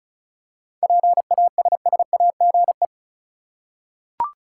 Morse code, heard as a keyed tone, about 34 wpm, 700 Hz. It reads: PASSAGE
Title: Seven Letter Words - 35wpm